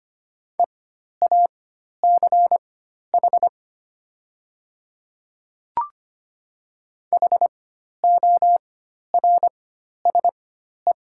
Morse code, heard as a keyed tone, about 25 wpm, spaced as for 12 wpm, 700 Hz. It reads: EACH HORSE
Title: 4x Character Spacing - Top 300 Words - 25wpm